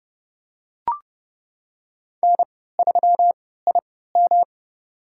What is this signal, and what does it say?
Morse code, heard as a keyed tone, about 32 wpm, spaced as for 17 wpm, 700 Hz. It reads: N3IM
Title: USA Call Signs - 4 Characters - 3x Character Spacing - 30wpm